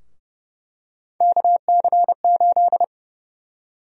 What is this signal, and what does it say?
Morse code, heard as a keyed tone, about 30 wpm, 700 Hz. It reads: KC8
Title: USA Call Sign Prefixes - Repeated 2x - 30wpm